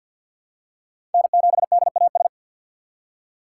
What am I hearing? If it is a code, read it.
Morse code, heard as a keyed tone, about 49 wpm, 700 Hz. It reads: N7DAS
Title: USA Call Signs - 5 Characters - 6x Character Spacing - 50wpm